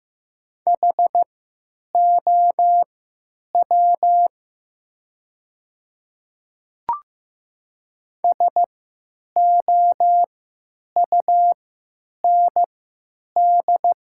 Morse code, heard as a keyed tone, about 15 wpm, spaced as for 9 wpm, 690 Hz. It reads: HOW SOUND
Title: 3x Character Spacing - Top 100 Words - 15wpm